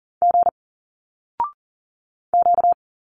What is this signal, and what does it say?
Morse code, heard as a keyed tone, about 40 wpm, 700 Hz. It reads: G Q
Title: Single Letters 40wpm